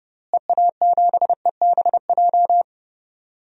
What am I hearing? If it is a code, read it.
Morse code, heard as a keyed tone, about 30 wpm, 700 Hz. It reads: EA7EBJ
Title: DXCC-100 Call Signs - 6 Characters - Repeated 2x - 30wpm